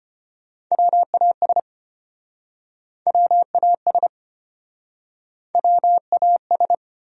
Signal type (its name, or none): Morse code